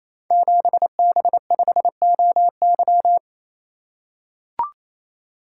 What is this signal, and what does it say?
Morse code, heard as a keyed tone, about 29 wpm, 700 Hz. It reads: 7B5OY
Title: Code Groups - 5 Letter:Numbers - 28wpm